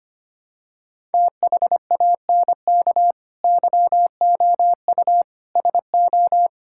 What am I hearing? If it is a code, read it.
Morse code, heard as a keyed tone, about 25 wpm, 700 Hz. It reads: THANK YOU SO
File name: Sets of 9 Words - 25wpm